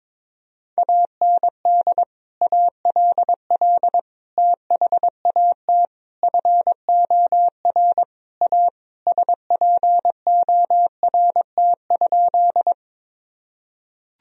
Morse code, heard as a keyed tone, about 22 wpm, 700 Hz. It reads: AND ALL THAT FOR A SPORT?
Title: Sentences from Top 1000 Words - Review Part 4 - 22wpm